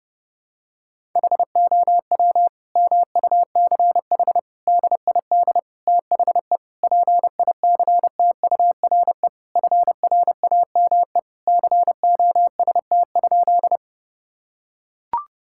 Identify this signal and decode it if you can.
Morse code, heard as a keyed tone, about 30 wpm, 700 Hz. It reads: HOW MUCH DID THE PICTURE FRAME COST?
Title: Sentences from Top 1000 Words - Review Part 6 - 30wpm